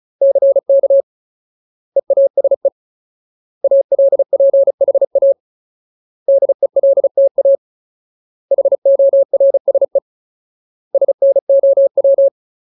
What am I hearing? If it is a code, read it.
Morse code, heard as a keyed tone, about 34 wpm, 550 Hz. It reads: CK EASE ALPHA DELTA HORSE SNOW
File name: Ultimate Word List / 35WPM Ultimate